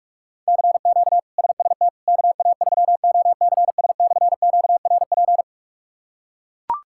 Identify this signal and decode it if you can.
Morse code, heard as a keyed tone, about 42 wpm, 700 Hz. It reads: CQ SST KA3OCS/QRP
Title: K1USN SST QSO Practice - 45wpm